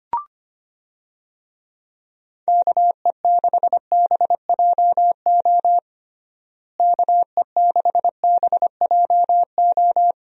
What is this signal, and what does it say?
Morse code, heard as a keyed tone, about 25 wpm, 700 Hz. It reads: KE6BJO KE6BJO
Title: USA Call Signs - 6 Characters - Repeated 3x - 25wpm